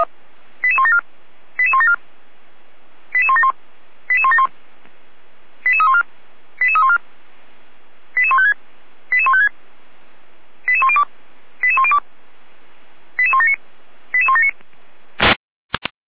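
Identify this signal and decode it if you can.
Five-tone selective call (ZVEI), heard as quick runs of five tones, tones of about 70 ms. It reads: 88174 88174 88171 88171 88314 88314 88156 88156 88102 88102 88179 88179
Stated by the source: ZVEI five-tone call received on SDR.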